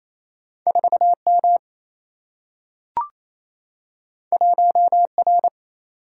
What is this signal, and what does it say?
Morse code, heard as a keyed tone, about 28 wpm, 700 Hz. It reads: 4 M 1 R
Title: Single-Letter Single-Number 28wpm